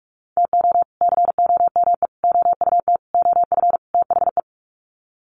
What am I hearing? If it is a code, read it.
Morse code, heard as a keyed tone, about 44 wpm, 700 Hz. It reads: TO COME OUT OF THE
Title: Sets of 5 Words Encore2 45wpm